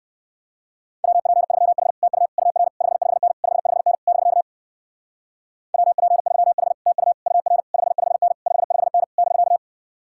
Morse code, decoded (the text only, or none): WP3H TU UR 55N 55N <BK> WP3H TU UR 55N 55N <BK>